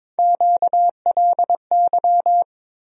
Morse code, heard as a keyed tone, about 23 wpm, 700 Hz. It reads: QLY
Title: USA Call Sign Postfixes - 3 Letters 22wpm